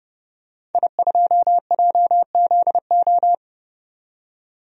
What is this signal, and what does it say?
Morse code, heard as a keyed tone, about 30 wpm, 700 Hz. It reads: I2JZO
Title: DXCC-100 Call Signs - 5 Characters - 30wpm